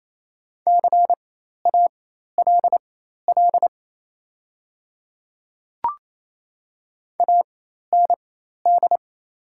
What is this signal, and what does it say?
Morse code, heard as a keyed tone, about 28 wpm, spaced as for 13 wpm, 700 Hz. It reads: CALL AND
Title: 4x Character Spacing - Top 100 Words - 28wpm